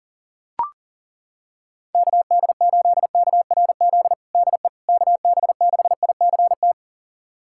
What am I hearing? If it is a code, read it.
Morse code, heard as a keyed tone, about 40 wpm, 700 Hz. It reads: KD8KRZ DE KB6ICT
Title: USA Callsign DE Callsign 40wpm